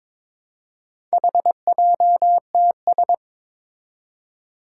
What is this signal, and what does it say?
Morse code, heard as a keyed tone, about 22 wpm, 700 Hz. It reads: HJTS
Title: Four Challenging Letter-Number Combinations 22wpm